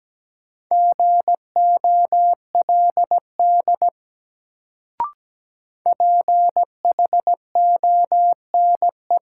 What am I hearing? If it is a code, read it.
Morse code, heard as a keyed tone, about 17 wpm, 700 Hz. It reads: GOLD PHONE
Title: Sentences from Top 1100 Words - New Word Drill Alt - 17wpm